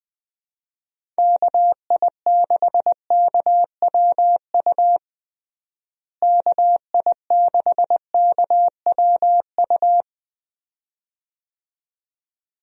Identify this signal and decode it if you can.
Morse code, heard as a keyed tone, about 20 wpm, 700 Hz. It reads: KI6KWU KI6KWU
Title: USA Call Signs - 6 Characters - Repeated 2x - 20wpm